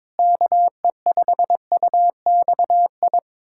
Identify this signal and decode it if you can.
Morse code, heard as a keyed tone, about 22 wpm, 700 Hz. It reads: KE5UXI
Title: USA Callsign DE Callsign 22wpm